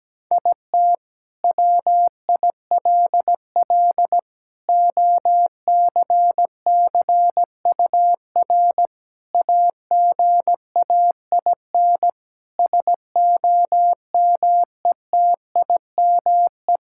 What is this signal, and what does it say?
Morse code, heard as a keyed tone, about 17 wpm, 700 Hz. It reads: IT WILL OCCUR AGAIN SOMETIME
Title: Sentences from Top 1000 Words - Review Part 6 - 17wpm